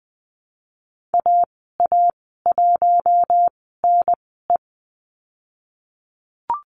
Morse code, heard as a keyed tone, about 20 wpm, spaced as for 15 wpm, 700 Hz. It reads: AA1NE
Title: 055 Lesson 10g - Letter r - Call Signs 20wpm